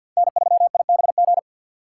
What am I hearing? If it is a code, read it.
Morse code, heard as a keyed tone, about 50 wpm, 700 Hz. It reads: N3IBG
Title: USA Call Signs - 5 Characters - 3x Character Spacing - 50wpm